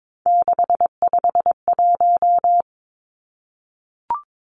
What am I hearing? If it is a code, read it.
Morse code, heard as a keyed tone, about 22 wpm, 700 Hz. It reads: 651
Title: Numbers - 3-Digit 22wpm